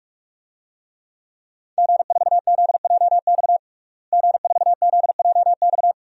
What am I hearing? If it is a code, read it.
Morse code, heard as a keyed tone, about 46 wpm, 700 Hz. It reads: G4ZJX G4ZJX
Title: DXCC-100 Call Signs - 5 Characters - Repeated 3x - 45wpm